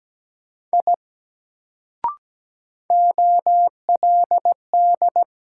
Morse code, heard as a keyed tone, about 17 wpm, 700 Hz. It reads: I OLD
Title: Sentences from Top 200 Words - New Word Drill - 17wpm